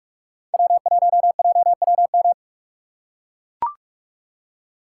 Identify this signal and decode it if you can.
Morse code, heard as a keyed tone, about 43 wpm, 700 Hz. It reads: W1JWM
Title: USA Call Signs - 2x Character Spacing - 45wpm